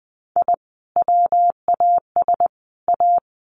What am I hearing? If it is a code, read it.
Morse code, heard as a keyed tone, about 20 wpm, 700 Hz. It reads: I WAS A
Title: Sets of 3 Words 20wpm